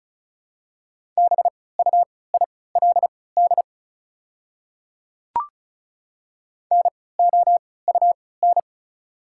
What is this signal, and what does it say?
Morse code, heard as a keyed tone, about 36 wpm, spaced as for 20 wpm, 690 Hz. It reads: BUILD NOUN
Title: 3x Character Spacing - Top 500 Words - 35wpm